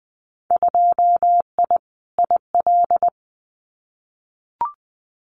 Morse code, heard as a keyed tone, about 20 wpm, 700 Hz. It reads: 2I IL
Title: Winter Field Day Exchanges 20wpm